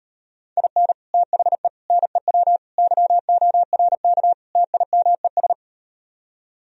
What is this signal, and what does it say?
Morse code, heard as a keyed tone, about 38 wpm, 700 Hz. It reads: IN THE NEW YORK TIMES